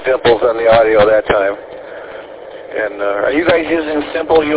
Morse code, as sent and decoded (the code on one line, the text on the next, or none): none